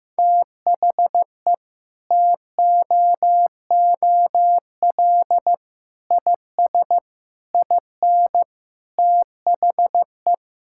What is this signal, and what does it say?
Morse code, heard as a keyed tone, about 16 wpm, 700 Hz. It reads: THE TOOL IS IN THE
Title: Sentences from Top 1000 Words - Review Part 6 - 15wpm